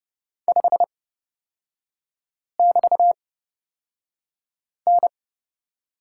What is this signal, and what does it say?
Morse code, heard as a keyed tone, about 30 wpm, spaced as for 5 wpm, 700 Hz. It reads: 5<BT>N
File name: Lesson 36e - BT - Three Characters 30wpm